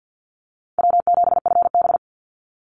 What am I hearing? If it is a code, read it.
Morse code, heard as a keyed tone, about 48 wpm, 700 Hz. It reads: W7FB
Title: Easy Call Signs Repeated 3x - 50wpm